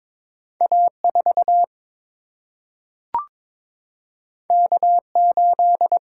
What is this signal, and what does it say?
Morse code, heard as a keyed tone, about 22 wpm, 700 Hz. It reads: A4 K8
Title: Single-Letter Single-Number 22wpm